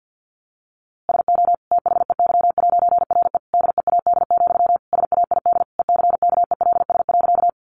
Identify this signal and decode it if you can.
Morse code, heard as a keyed tone, about 50 wpm, 700 Hz. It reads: SO THEY'RE DEAD, SAID ECKELS.
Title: A Sound Of Thunder 50wpm